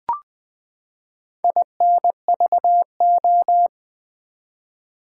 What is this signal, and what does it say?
Morse code, heard as a keyed tone, about 20 wpm, 700 Hz. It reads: INVO
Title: Build A Word - 4 Letters At A Time - 11-Letter Words - 20wpm